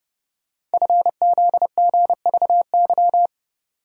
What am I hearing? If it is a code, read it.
Morse code, heard as a keyed tone, about 30 wpm, 700 Hz. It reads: FZGVY
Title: Code Groups - 5 Letters - 30wpm